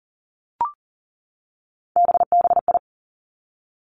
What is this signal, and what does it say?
Morse code, heard as a keyed tone, about 38 wpm, 700 Hz. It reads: BBI
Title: Three Challenging Letter-Number Combinations 40wpm